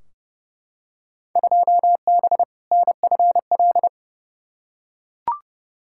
Morse code, heard as a keyed tone, about 30 wpm, 700 Hz. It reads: 2B NFL
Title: Field Day Exchanges Encore - 30wpm